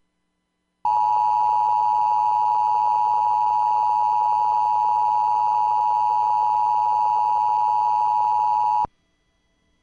Emergency Alert System (SAME) tones (the 853 and 960 Hz attention tone). It is heard from 0.8 to 8.9 s.